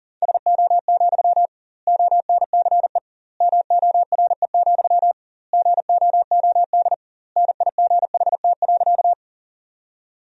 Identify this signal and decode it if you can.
Morse code, heard as a keyed tone, about 40 wpm, 700 Hz. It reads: SO, ONCE MORE, GOOD NIGHT.
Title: 2x Word-Spacing - Sentences from Top 300 Words - 40wpm